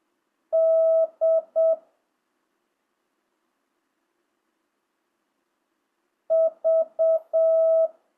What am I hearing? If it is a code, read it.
Morse code, heard as a keyed tone, about 7 wpm, 650 Hz. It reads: D V
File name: jh8cbh / 2025cw